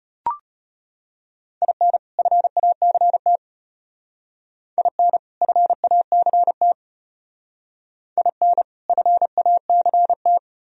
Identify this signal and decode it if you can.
Morse code, heard as a keyed tone, about 29 wpm, 700 Hz. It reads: IN FACT IN FACT IN FACT
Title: Speed-Racing - Sets of 2 Words 25wpm